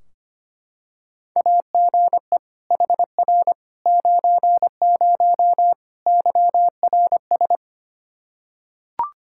Morse code, heard as a keyed tone, about 25 wpm, 700 Hz. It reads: AGE HR 90 YRS
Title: Practice QSO - Exchange 3 - More Info - 25wpm